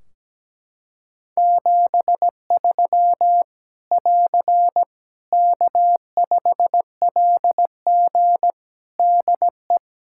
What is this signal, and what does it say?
Morse code, heard as a keyed tone, about 17 wpm, 700 Hz. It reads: 73 <AR> K5LG DE
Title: K1USN SST QSO Practice 17wpm